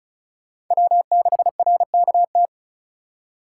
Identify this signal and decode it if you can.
Morse code, heard as a keyed tone, about 46 wpm, 700 Hz. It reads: W6RKT